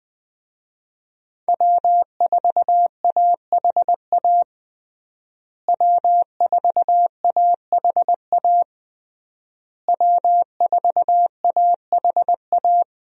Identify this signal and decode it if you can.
Morse code, heard as a keyed tone, about 20 wpm, 700 Hz. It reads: W4AHA W4AHA W4AHA
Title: USA Call Signs - 5 Characters - Repeated 3x - 20wpm